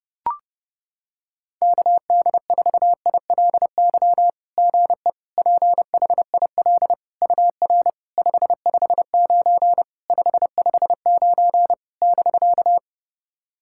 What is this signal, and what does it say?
Morse code, heard as a keyed tone, about 28 wpm, 690 Hz. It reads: KD4ILY GE PHIL UR 559 559 <BK>
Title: SOTA QSO Practice 30wpm